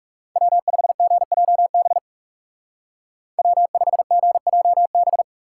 Morse code, heard as a keyed tone, about 36 wpm, 700 Hz. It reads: W5GJB W5GJB
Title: Speed Racing - USA 1x3 Call Signs 30wpm